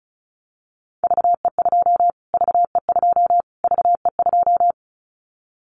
Morse code, heard as a keyed tone, about 34 wpm, 700 Hz. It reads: VE2 VE2 VE2
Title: Callsign Prefixes for Top 25 DXCC - Repeated 3x - 35wpm